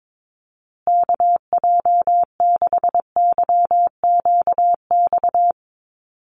Morse code, heard as a keyed tone, about 22 wpm, 700 Hz. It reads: KJ6YQX
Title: USA Call Signs Repeated 3x 22wpm